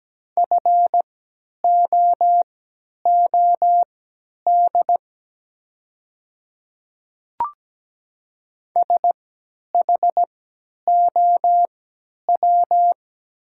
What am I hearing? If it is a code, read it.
Morse code, heard as a keyed tone, about 17 wpm, spaced as for 10 wpm, 700 Hz. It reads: FOOD SHOW
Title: 3x Character Spacing - Top 500 Words - 17wpm